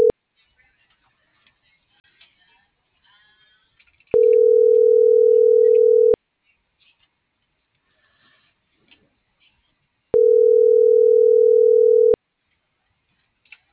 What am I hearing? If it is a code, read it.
Ringback tone (North America), 440 + 480 Hz.